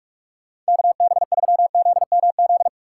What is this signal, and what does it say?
Morse code, heard as a keyed tone, about 54 wpm, 700 Hz. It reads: KB3ZMZ
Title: Speed Racing - USA 2x3 Call Signs 45wpm